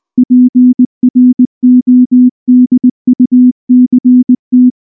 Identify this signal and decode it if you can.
Morse code, heard as a keyed tone, about 20 wpm, 250 Hz. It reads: PRODUCT
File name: morse / morse-training-20WPM-7